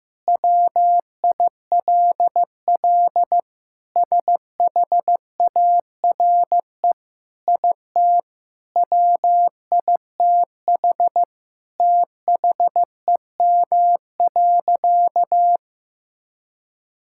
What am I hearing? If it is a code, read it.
Morse code, heard as a keyed tone, about 16 wpm, 700 Hz. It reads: WILL SHARE IT WITH THEM.
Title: Sentences from Top 1000 Words - Review Part 6 - 15wpm